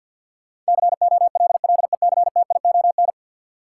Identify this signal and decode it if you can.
Morse code, heard as a keyed tone, about 50 wpm, 700 Hz. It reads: COLLECTION
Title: Sentences from Top 1200 Words - New Word Drill Alt - 50wpm